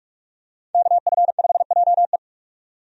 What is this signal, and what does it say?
Morse code, heard as a keyed tone, about 47 wpm, 700 Hz. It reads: KF5JE
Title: USA Call Signs - 5 Characters - 45wpm